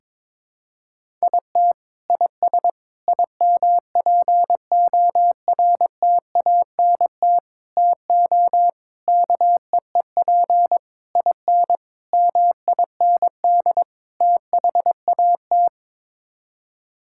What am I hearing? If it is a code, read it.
Morse code, heard as a keyed tone, about 22 wpm, 700 Hz. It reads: IT IS IMPORTANT TO KEEP IN MIND THAT
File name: practice / 22wpm / Sets of 8 Words - 22wpm